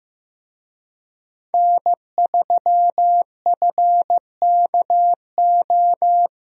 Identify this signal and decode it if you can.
Morse code, heard as a keyed tone, about 15 wpm, 700 Hz. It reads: N3FKO
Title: USA Call Signs - 5 Characters - Repeated 2x - 15wpm